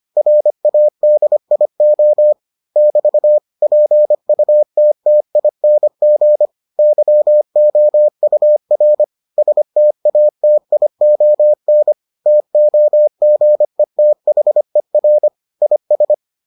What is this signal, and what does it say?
Morse code, heard as a keyed tone, about 25 wpm, 600 Hz. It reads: RADIO <BT> PUTTING YOUR STATION TOGETHER IS